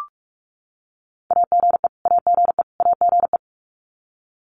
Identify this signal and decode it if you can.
Morse code, heard as a keyed tone, about 44 wpm, 700 Hz. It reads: AGE AGE AGE
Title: Top 500 Words Repeated 45wpm